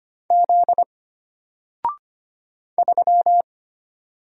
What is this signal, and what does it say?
Morse code, heard as a keyed tone, about 25 wpm, 700 Hz. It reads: Z 3
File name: Single-Letter-Number-25wpm